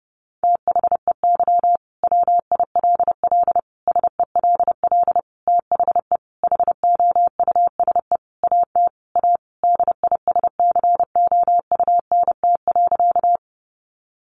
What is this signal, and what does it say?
Morse code, heard as a keyed tone, about 30 wpm, 700 Hz. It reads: THEY WILL SELL THE HOUSE AT A DISCOUNT.
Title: Sentences from Top 1200 Words 30wpm